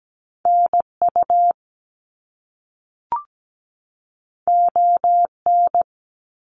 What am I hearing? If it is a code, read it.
Morse code, heard as a keyed tone, about 17 wpm, 700 Hz. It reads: NU ON
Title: Canadian Provinces and Territories 17wpm